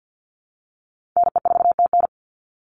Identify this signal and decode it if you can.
Morse code, heard as a keyed tone, about 50 wpm, 700 Hz. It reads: NE4TN
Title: NA SOTA Chaser Callsigns 50wpm